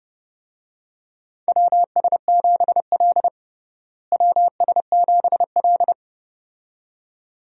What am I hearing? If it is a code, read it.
Morse code, heard as a keyed tone, about 30 wpm, 700 Hz. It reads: WS7L WS7L
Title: USA Call Signs - 4 Characters - Repeated 2x - 30wpm